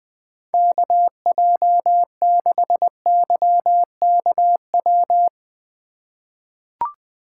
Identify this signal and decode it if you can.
Morse code, heard as a keyed tone, about 20 wpm, 700 Hz. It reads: KJ6YKW
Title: USA Call Signs - 6 Characters - 3x Character Spacing - 20wpm